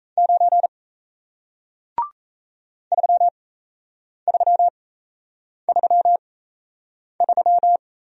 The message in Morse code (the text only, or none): 9 3333